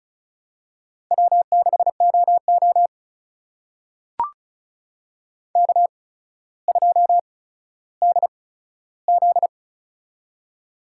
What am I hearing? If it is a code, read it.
Morse code, heard as a keyed tone, about 36 wpm, spaced as for 15 wpm, 700 Hz. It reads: W6OO K2DZ
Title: USA Call Signs - 4 Characters - 8x Character Spacing - 35wpm